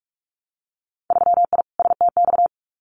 Morse code, heard as a keyed tone, about 46 wpm, 700 Hz. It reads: 3I STX
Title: Winter Field Day Exchanges Encore 45wpm